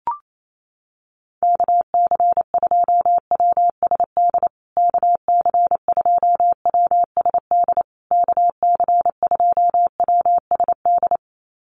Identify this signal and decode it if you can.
Morse code, heard as a keyed tone, about 28 wpm, 700 Hz. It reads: KC2WSD KC2WSD KC2WSD
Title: USA Call Signs Repeated 3x 28wpm